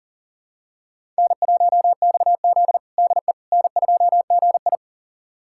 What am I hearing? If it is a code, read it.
Morse code, heard as a keyed tone, about 37 wpm, 700 Hz. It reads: N1XZ DE N2GI
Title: USA Callsign DE Callsign - 4 Characters - 40wpm